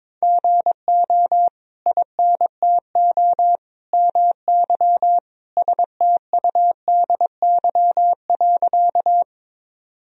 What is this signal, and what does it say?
Morse code, heard as a keyed tone, about 22 wpm, 700 Hz. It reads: GO INTO MY STUDY.
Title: Sentences from Top 1000 Words - Review Part 1 - 22wpm